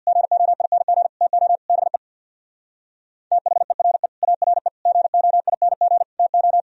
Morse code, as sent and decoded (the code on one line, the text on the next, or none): --. --- .. -. --. / - --- / -... / - .... .-. / .- .-. / --. --- .. -. --. / - ---
GOING TO BE THERE ARE GOING TO